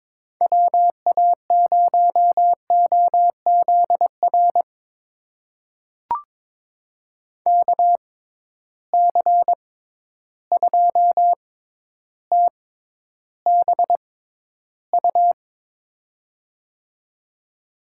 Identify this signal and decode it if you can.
Morse code, heard as a keyed tone, about 21 wpm, spaced as for 11 wpm, 700 Hz. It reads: WA0OZR KC2TBU
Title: USA Call Signs - 6 Characters - 6x Character Spacing - 22wpm